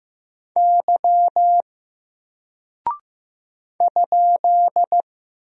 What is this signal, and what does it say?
Morse code, heard as a keyed tone, about 15 wpm, 700 Hz. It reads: Y ?
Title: Single Character (Letter:Number:Punctuation:Prosign) 15wpm